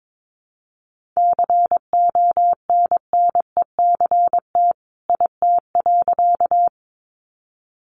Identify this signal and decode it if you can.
Morse code, heard as a keyed tone, about 22 wpm, 700 Hz. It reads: CONNECT IT.
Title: Sentences from Top 1500 Words 22wpm